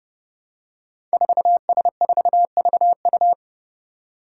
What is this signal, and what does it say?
Morse code, heard as a keyed tone, about 30 wpm, 700 Hz. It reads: 4S4VU
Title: Five Challenging Letter-Number Combinations 30wpm